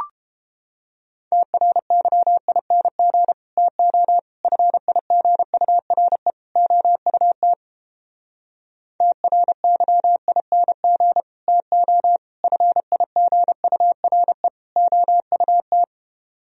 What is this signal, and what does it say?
Morse code, heard as a keyed tone, about 29 wpm, 700 Hz. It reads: TRYING TO FIGURE OUT TRYING TO FIGURE OUT